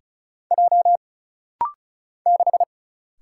Morse code, heard as a keyed tone, about 36 wpm, 700 Hz. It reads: J 6
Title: Single Letter-Number - Rapid-Fire 35wpm